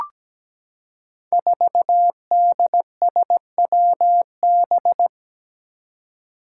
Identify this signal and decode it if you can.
Morse code, heard as a keyed tone, about 17 wpm, 700 Hz. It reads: 4DSWB
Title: Five Challenging Letter-Number Combinations 17wpm